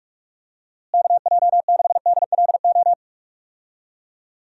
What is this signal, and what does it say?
Morse code, heard as a keyed tone, about 43 wpm, 700 Hz. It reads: KJ6DLO
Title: USA Call Signs - 6 Characters - Repeated 3x - 45wpm